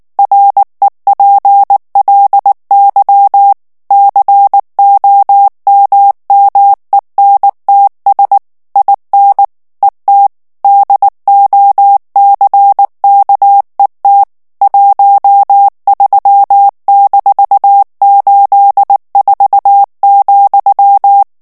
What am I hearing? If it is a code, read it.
Morse code, heard as a keyed tone, about 19 wpm, 800 Hz. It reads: REPLY COMMENTS IN ET DOCKET 13-84,
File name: ANGLAIS texte clair 3 vitesse 18 mots minute